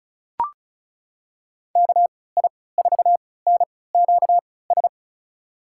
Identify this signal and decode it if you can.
Morse code, heard as a keyed tone, about 35 wpm, spaced as for 20 wpm, 700 Hz. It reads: KI4NQS
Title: USA Call Signs - 6 Characters - 3x Character Spacing - 35wpm